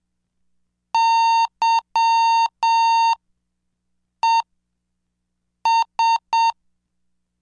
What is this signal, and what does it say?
Morse code, heard as a keyed tone, about 7 wpm, spaced as for 5 wpm, 920 Hz. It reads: YES
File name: cursos / CDestudo / 20 Track 20